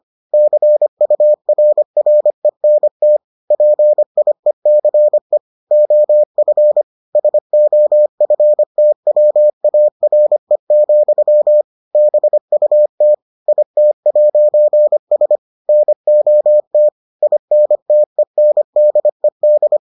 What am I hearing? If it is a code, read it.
Morse code, heard as a keyed tone, about 25 wpm, 600 Hz. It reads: CURRENT PIECE OF SOFTWARE, BUT IT'S NOT INTENDED